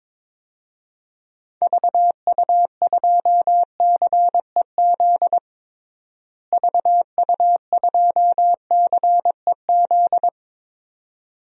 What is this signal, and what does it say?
Morse code, heard as a keyed tone, about 22 wpm, 700 Hz. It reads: VU2CEZ VU2CEZ
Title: DXCC-100 Call Signs - 6 Characters - Repeated 2x - 22wpm